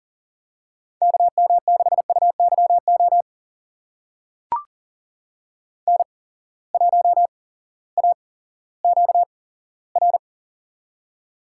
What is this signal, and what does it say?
Morse code, heard as a keyed tone, about 42 wpm, spaced as for 19 wpm, 700 Hz. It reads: KM6UYO N1AQR
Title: USA Call Signs - 8x Character Spacing - 40wpm